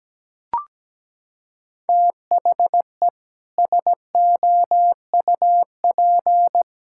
Morse code, heard as a keyed tone, about 17 wpm, 700 Hz. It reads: THE SOUP
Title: Sentences from Top 1000 Words - Review Part 6 - 17wpm